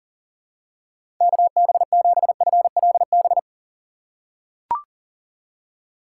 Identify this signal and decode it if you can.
Morse code, heard as a keyed tone, about 40 wpm, 700 Hz. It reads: KB7FLB
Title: USA Call Signs - 6 Characters - 2x Character Spacing - 40wpm